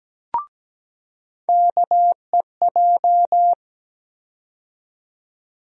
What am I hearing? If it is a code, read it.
Morse code, heard as a keyed tone, about 17 wpm, 700 Hz. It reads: KEJ
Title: USA Call Sign Postfixes - 3 Letters 17wpm